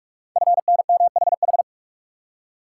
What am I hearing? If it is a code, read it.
Morse code, heard as a keyed tone, about 44 wpm, 700 Hz. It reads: FNMHH
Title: Code Groups - 5 Letters - 45wpm